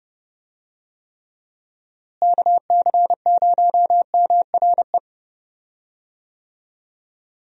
Morse code, heard as a keyed tone, about 30 wpm, 700 Hz. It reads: KC0MRE
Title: USA Call Signs - 6 Characters - Repeated 2x - 30wpm